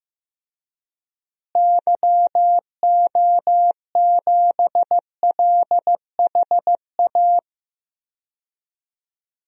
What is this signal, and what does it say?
Morse code, heard as a keyed tone, about 15 wpm, 700 Hz. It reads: YO7LHA
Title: DXCC-100 Call Signs - 6 Characters - Repeated 3x - 15wpm